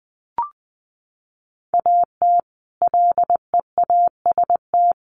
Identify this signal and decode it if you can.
Morse code, heard as a keyed tone, about 20 wpm, 700 Hz. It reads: AT LEAST
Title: Sentences from Top 1400 Words 20wpm